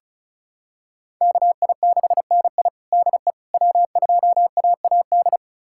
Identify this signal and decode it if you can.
Morse code, heard as a keyed tone, about 36 wpm, 700 Hz. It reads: KI6NI DE W2AAD
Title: USA Callsign DE Callsign - 5 Characters - 35wpm